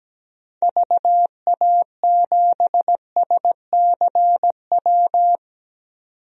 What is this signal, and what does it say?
Morse code, heard as a keyed tone, about 17 wpm, 700 Hz. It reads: VA7SCW
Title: DXCC-100 Call Signs - 6 Characters - 17wpm